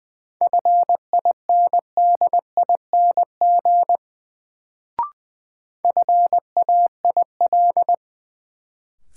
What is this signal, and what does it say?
Morse code, heard as a keyed tone, about 20 wpm, 700 Hz. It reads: FINDING FAIL
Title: Sentences from Top 1100 Words - New Word Drill Alt - 20wpm